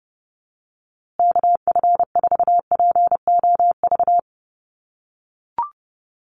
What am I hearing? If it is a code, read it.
Morse code, heard as a keyed tone, about 30 wpm, 700 Hz. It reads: KF4POV
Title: USA Call Signs Repeated 3x 30wpm